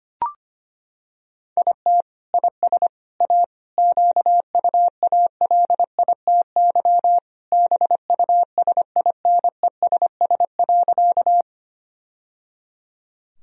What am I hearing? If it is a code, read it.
Morse code, heard as a keyed tone, about 25 wpm, 690 Hz. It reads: IT IS A QUALITY BUSINESS.
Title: Sentences from Top 1000 Words - Review Part 4 - 25wpm